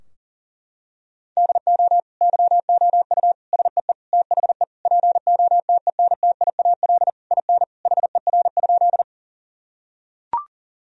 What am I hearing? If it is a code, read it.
Morse code, heard as a keyed tone, about 40 wpm, 700 Hz. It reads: DO YOU SEE THE POTENTIAL IN HER?
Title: Sentences from Top 1000 Words - Review Part 4 - 40wpm